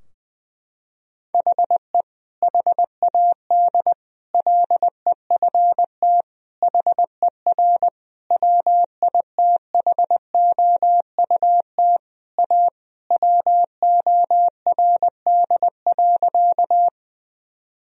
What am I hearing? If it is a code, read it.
Morse code, heard as a keyed tone, about 21 wpm, 700 Hz. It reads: HE HAD LEFT HER WITHOUT A WORD.
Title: Sentences from Top 1000 Words - Review Part 3 - 20wpm